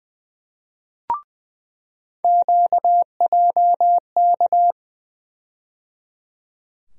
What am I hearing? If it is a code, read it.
Morse code, heard as a keyed tone, about 20 wpm, 700 Hz. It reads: QJK
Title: USA Call Sign Postfixes - 3 Letters 20wpm